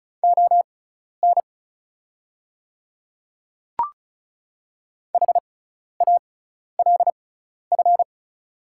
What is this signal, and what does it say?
Morse code, heard as a keyed tone, about 36 wpm, spaced as for 12 wpm, 690 Hz. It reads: ON HALF
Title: 6x Character Spacing - Top 500 Words - 35wpm